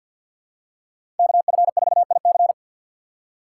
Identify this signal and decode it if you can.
Morse code, heard as a keyed tone, about 50 wpm, 700 Hz. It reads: KF4IC
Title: USA Call Signs - 5 Characters - 4x Character Spacing - 50wpm